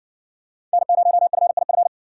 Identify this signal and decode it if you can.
Morse code, heard as a keyed tone, about 60 wpm, 700 Hz. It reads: N0FIF